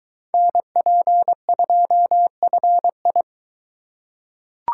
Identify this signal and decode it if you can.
Morse code, heard as a keyed tone, about 21 wpm, 700 Hz. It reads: NP2FI